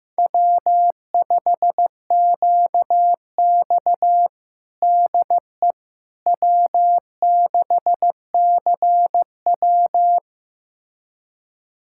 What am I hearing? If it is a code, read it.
Morse code, heard as a keyed tone, about 16 wpm, 700 Hz. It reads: W5QX DE W6CW
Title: USA Callsign DE Callsign - 4 Characters - 15wpm